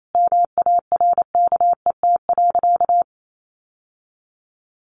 Morse code, heard as a keyed tone, about 28 wpm, 700 Hz. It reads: MARKET.
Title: Sentences from Top 1400 Words 28wpm